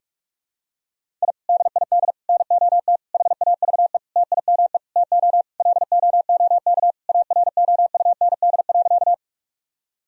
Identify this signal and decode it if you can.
Morse code, heard as a keyed tone, about 46 wpm, 700 Hz. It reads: I DID NOT HAVE TIME TO LOOK AROUND.
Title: Sentences from Top 1000 Words - Review Part 3 - 45wpm